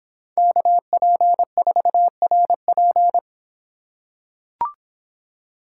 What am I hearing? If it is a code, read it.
Morse code, heard as a keyed tone, about 26 wpm, 700 Hz. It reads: KP4RP